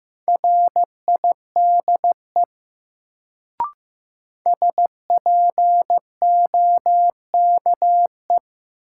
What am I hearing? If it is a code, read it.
Morse code, heard as a keyed tone, about 15 wpm, 700 Hz. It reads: RIDE SPOKE
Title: Sentences from Top 1200 Words - New Word Drill Alt - 15wpm